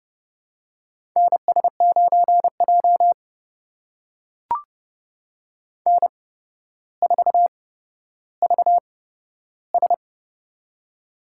Morse code, heard as a keyed tone, about 30 wpm, spaced as for 13 wpm, 700 Hz. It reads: NS9J N4VS
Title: USA Call Signs - 4 Characters - 8x Character Spacing - 30wpm